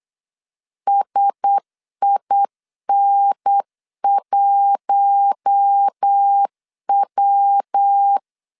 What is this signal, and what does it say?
Morse code, heard as a keyed tone, about 8 wpm, 800 Hz. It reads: SIN1W
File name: LEZIONI / 6 PKX0 / 6 P 8wpm 3025